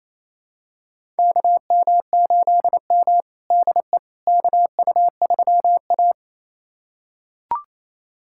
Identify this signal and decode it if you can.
Morse code, heard as a keyed tone, about 28 wpm, 700 Hz. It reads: KM8M DE KU3A
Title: USA Callsign DE Callsign - 4 Characters - 28wpm